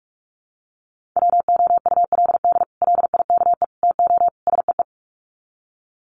Morse code, heard as a keyed tone, about 44 wpm, 700 Hz. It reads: WOULD LIKE TO SEE
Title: Sets of 4 Words 45wpm